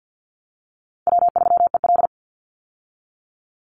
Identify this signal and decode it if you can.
Morse code, heard as a keyed tone, about 50 wpm, 700 Hz. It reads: W3EL
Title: Easy Call Signs Repeated 2x - 50wpm